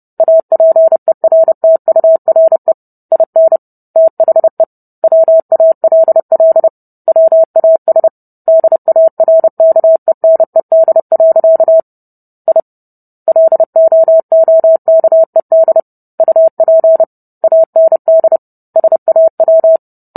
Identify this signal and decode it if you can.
Morse code, heard as a keyed tone, about 30 wpm, 650 Hz. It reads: APERTURE IN THE WALL WAS DARKENED. I LOOKED UP AND SAW